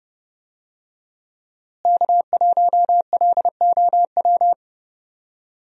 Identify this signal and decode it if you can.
Morse code, heard as a keyed tone, about 30 wpm, 700 Hz. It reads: K1LOW
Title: USA Call Signs - 5 Characters - Repeated 3x - 30wpm